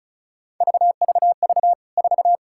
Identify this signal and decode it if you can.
Morse code, heard as a keyed tone, about 36 wpm, 700 Hz. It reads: VVV 4
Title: K1USN SST QSO Practice 35wpm